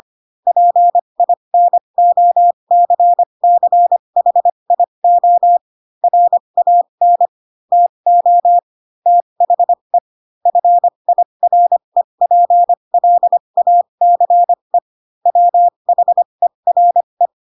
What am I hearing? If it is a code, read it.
Morse code, heard as a keyed tone, about 25 wpm, spaced as for 22 wpm, 700 Hz. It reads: PINOCCHIO RAN TO THE FIREPLACE WHERE